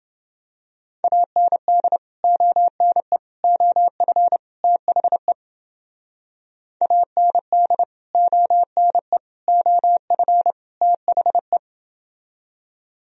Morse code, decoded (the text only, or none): AND ONE OF THE AND ONE OF THE